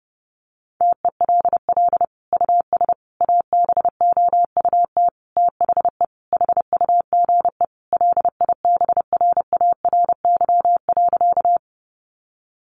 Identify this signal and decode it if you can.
Morse code, heard as a keyed tone, about 30 wpm, 700 Hz. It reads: TELL US ABOUT THE HUGE LIBRARY.
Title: Sentences from Top 1100 Words 30wpm